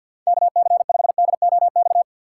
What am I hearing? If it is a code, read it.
Morse code, heard as a keyed tone, about 49 wpm, 700 Hz. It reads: KC5DOX
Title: USA Call Signs - 6 Characters - 6x Character Spacing - 50wpm